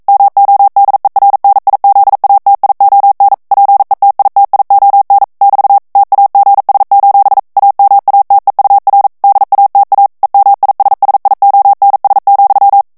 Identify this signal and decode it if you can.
Morse code, heard as a keyed tone, about 42 wpm, 800 Hz. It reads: MODERNIZATION PETITION <BT> TAGS: AMATEUR DATA EMISSIONS,
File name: 40 mots minute